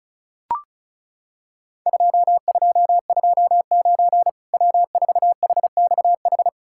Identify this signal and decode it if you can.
Morse code, heard as a keyed tone, about 34 wpm, 700 Hz. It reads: 2229 W4HXH
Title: Worked All Europe DX Contest - QTC Exchange - 35wpm